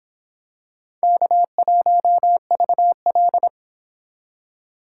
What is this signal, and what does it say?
Morse code, heard as a keyed tone, about 26 wpm, 700 Hz. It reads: K1VL